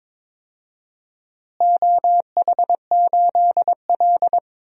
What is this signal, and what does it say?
Morse code, heard as a keyed tone, about 22 wpm, 700 Hz. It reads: OH8L
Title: DXCC-100 Call Signs - 4 Characters - Repeated 3x - 22wpm